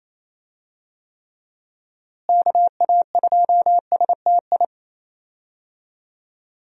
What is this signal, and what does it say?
Morse code, heard as a keyed tone, about 28 wpm, 700 Hz. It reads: KA2STI
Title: USA Call Signs - 6 Characters - Repeated 2x - 28wpm